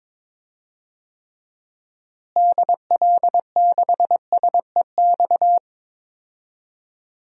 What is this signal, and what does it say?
Morse code, heard as a keyed tone, about 22 wpm, 700 Hz. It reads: DL6SEX
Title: DXCC-100 Call Signs - 6 Characters - Repeated 2x - 22wpm